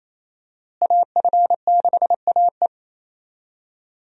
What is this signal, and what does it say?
Morse code, heard as a keyed tone, about 28 wpm, 700 Hz. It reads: AF6AE
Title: USA Call Signs - 5 Characters - Repeated 3x - 28wpm